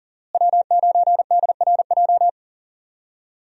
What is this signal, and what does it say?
Morse code, heard as a keyed tone, about 40 wpm, 700 Hz. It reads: W9DRJ
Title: USA Call Signs - 5 Characters - Repeated 3x - 40wpm